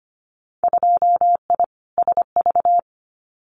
Morse code, heard as a keyed tone, about 25 wpm, 700 Hz. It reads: 2I SV
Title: Winter Field Day Exchanges Encore 25wpm